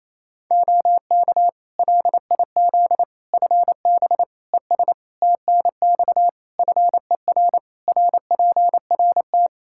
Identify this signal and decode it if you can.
Morse code, heard as a keyed tone, about 28 wpm, 700 Hz. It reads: OK LIZ FB ES TNX FER RPRT
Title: K1USN SST QSO Practice 28wpm